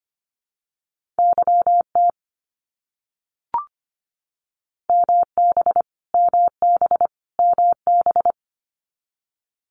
Morse code, heard as a keyed tone, about 25 wpm, 700 Hz. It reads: YT MB MB MB
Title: Canadian Provinces and Territories - Repeated 3x - 25wpm